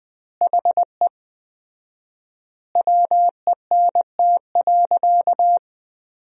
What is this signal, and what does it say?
Morse code, heard as a keyed tone, about 20 wpm, 700 Hz. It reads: HE WENT.
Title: 4x Word-Spacing - Sentences from Top 200 Words - 20wpm